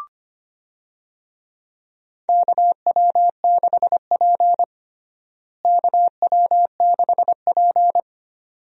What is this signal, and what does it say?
Morse code, heard as a keyed tone, about 25 wpm, 700 Hz. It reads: KW6P KW6P
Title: USA Call Signs - 4 Characters - Repeated 2x - 25wpm